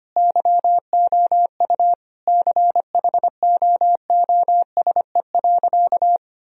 Morse code, heard as a keyed tone, about 25 wpm, 700 Hz. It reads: YOU CHOOSE.
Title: Sentences from Top 1000 Words - Review Part 4 - 25wpm